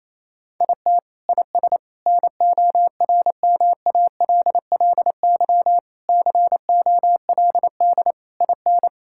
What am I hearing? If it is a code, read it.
Morse code, heard as a keyed tone, about 28 wpm, 700 Hz. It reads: IT IS NORMALLY COLD IN
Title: Sentences from Top 1000 Words - Review Part 6 - 28wpm